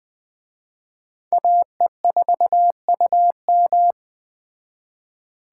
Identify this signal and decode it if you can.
Morse code, heard as a keyed tone, about 20 wpm, 700 Hz. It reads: AE4UM
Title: Speed Racing - USA 2x2 Call Signs 17wpm